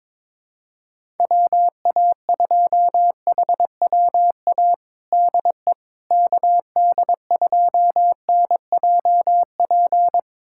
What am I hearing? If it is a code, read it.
Morse code, heard as a keyed tone, about 22 wpm, 700 Hz. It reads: WA2HWA DE KD2NJP
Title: USA Callsign DE Callsign - 6 Characters - 22wpm